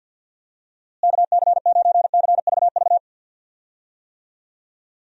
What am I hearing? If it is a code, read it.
Morse code, heard as a keyed tone, about 48 wpm, 700 Hz. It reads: KC9CVV
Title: USA Call Signs - 6 Characters - Repeated 3x - 50wpm